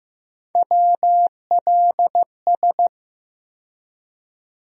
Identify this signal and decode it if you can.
Morse code, heard as a keyed tone, about 15 wpm, 700 Hz. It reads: WLS
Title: USA Call Sign Postfixes - 3 Letters 15wpm